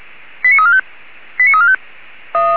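Five-tone selective call (ZVEI), heard as quick runs of five tones, tones of about 80 ms. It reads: 89345 89345
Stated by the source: ZVEI five-tone selective call 89352 received on SDR.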